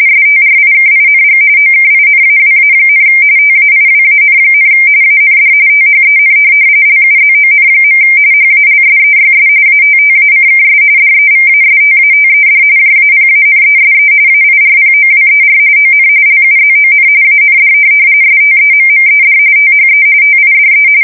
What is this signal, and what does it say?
RTTY, about 45.45 baud. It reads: STRIKING OKINAWA, TAIWAN, AND THEN LUZON. HALSEYS ATTACKS DEVASTATED JAPANESE AIRPOWER ON LUZON AND STEMMED THE TIDE
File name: fsk170 / USS-New-Jersey